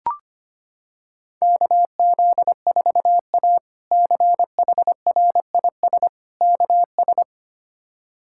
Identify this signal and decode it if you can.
Morse code, heard as a keyed tone, about 25 wpm, 700 Hz. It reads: KZ4A CHRIS KS
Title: K1USN SST QSO Practice - 25wpm